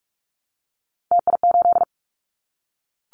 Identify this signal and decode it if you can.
Morse code, heard as a keyed tone, about 44 wpm, 700 Hz. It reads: TI8
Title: Callsign Prefixes for Top 100 DXCC - 45wpm